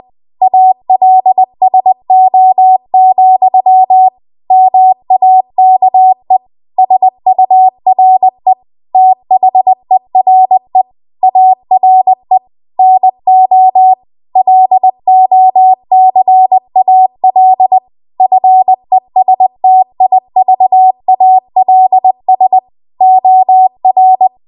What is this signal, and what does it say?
Morse code, heard as a keyed tone, about 20 wpm, 750 Hz. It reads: ALSO, MAKE SURE THERE ARE NO LOCAL FESTIVALS OR